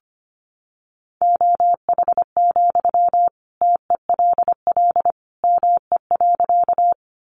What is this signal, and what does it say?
Morse code, heard as a keyed tone, about 25 wpm, 700 Hz. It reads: OH, TELL ME.
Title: Sentences from Top 500 Words 25wpm